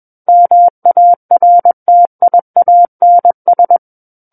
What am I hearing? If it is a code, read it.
Morse code, heard as a keyed tone, about 21 wpm, 700 Hz. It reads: MARTIANS
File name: Book One / 12-Chapter 12 - 21 WPM